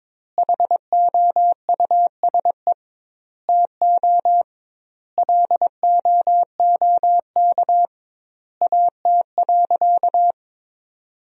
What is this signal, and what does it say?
Morse code, heard as a keyed tone, about 22 wpm, 700 Hz. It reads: HOUSE TO LOOK AT.
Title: 2x Word-Spacing - Sentences from Top 200 Words - 22wpm